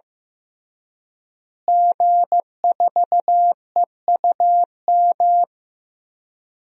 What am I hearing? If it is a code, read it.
Morse code, heard as a keyed tone, about 16 wpm, 700 Hz. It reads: G4EUM
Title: DXCC-100 Call Signs - 5 Characters - Repeated 3x - 15wpm